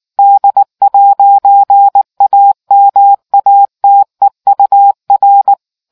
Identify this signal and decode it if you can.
Morse code, heard as a keyed tone, about 19 wpm, 800 Hz. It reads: D'AMATEUR